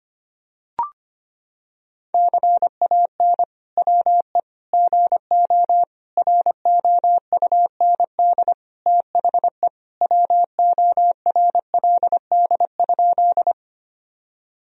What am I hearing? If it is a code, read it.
Morse code, heard as a keyed tone, about 25 wpm, 700 Hz. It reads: CAN WE GO ROUND THE WORLD?
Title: Sentences from Top 1000 Words - Review Part 1 - 25wpm